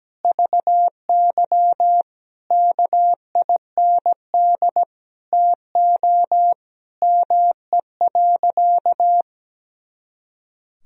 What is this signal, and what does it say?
Morse code, heard as a keyed tone, about 17 wpm, 700 Hz. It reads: VY KIND TO ME.
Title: Sentences from Top 200 Words and Abbreviations 17wpm